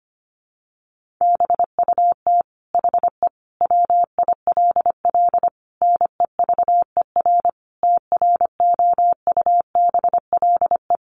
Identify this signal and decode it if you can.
Morse code, heard as a keyed tone, about 25 wpm, 700 Hz. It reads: BUT HE WILL NEVER TROUBLE
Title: Sentences from Top 1200 Words 25wpm